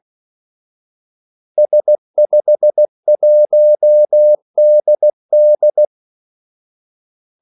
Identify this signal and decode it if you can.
Morse code, heard as a keyed tone, about 16 wpm, 600 Hz. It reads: S51DD
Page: Kaldesignaler 16 wpm